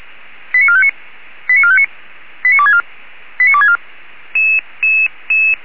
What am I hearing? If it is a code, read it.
Five-tone selective call (ZVEI), heard as quick runs of five tones, tones of about 70 ms. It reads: 89459 89459 89264 89264